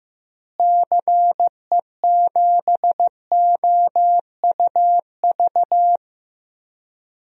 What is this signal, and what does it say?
Morse code, heard as a keyed tone, about 15 wpm, 700 Hz. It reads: CE7OUV
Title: DXCC-100 Call Signs - 6 Characters - Repeated 3x - 15wpm